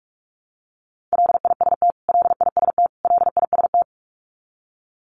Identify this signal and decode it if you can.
Morse code, heard as a keyed tone, about 44 wpm, 700 Hz. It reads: LIST LIST LIST
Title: Top 500 Words Repeated 45wpm